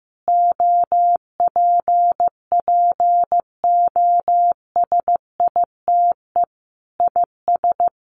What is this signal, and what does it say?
Morse code, heard as a keyed tone, about 15 wpm, 700 Hz. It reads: OPPOSITE IS
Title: Sentences from Top 1300 Words 15wpm